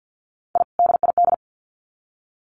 Morse code, heard as a keyed tone, about 50 wpm, 700 Hz. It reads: I DID
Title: Sets of 2 Words 50wpm